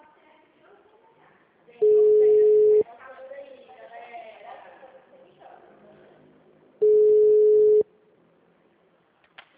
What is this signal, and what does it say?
Ringback tone (Europe, 425 Hz), with speech.
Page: Dracena (Ura-Reversa)